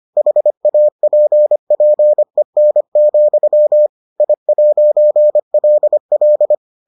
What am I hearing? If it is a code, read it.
Morse code, heard as a keyed tone, about 25 wpm, 600 Hz. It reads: HAPPEN, I'LL